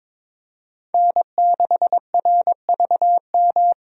Morse code, heard as a keyed tone, about 22 wpm, 700 Hz. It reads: N6RVM
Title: USA Call Signs - 5 Characters - 22wpm